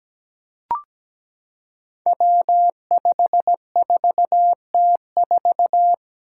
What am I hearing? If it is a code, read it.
Morse code, heard as a keyed tone, about 17 wpm, 700 Hz. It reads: W54T4
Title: Five Challenging Letter-Number Combinations 17wpm